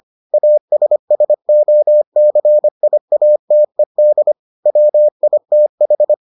Morse code, heard as a keyed tone, about 25 wpm, 600 Hz. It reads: ASSOCIATED WITH